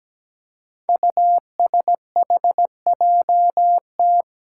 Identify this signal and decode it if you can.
Morse code, heard as a keyed tone, about 17 wpm, 700 Hz. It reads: USHJT
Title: Five Challenging Letter-Number Combinations 17wpm